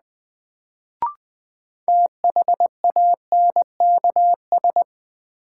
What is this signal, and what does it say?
Morse code, heard as a keyed tone, about 20 wpm, 700 Hz. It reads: THANKS
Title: Sets of 5 Words - New Word Drill - 20wpm